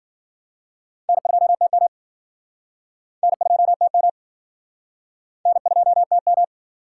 Morse code, heard as a keyed tone, about 56 wpm, 700 Hz. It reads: N2TM N2TM N2TM